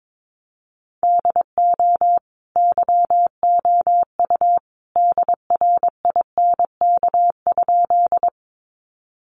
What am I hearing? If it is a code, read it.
Morse code, heard as a keyed tone, about 22 wpm, 700 Hz. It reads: DO YOU DRINK?
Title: Sentences from Top 1500 Words 22wpm